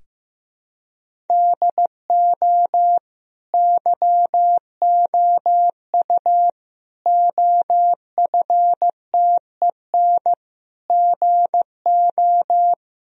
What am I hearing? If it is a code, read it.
Morse code, heard as a keyed tone, about 15 wpm, 700 Hz. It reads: DO YOU OFTEN GO
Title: Sentences from Top 1000 Words - Review Part 1 - 15wpm